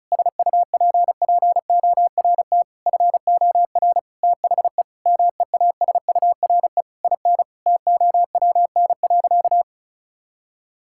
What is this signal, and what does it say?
Morse code, heard as a keyed tone, about 36 wpm, 700 Hz. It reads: SUPPORT FOR THE MEASURE IN TOWN.
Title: Sentences from Top 1000 Words - Review Part 5 - 35wpm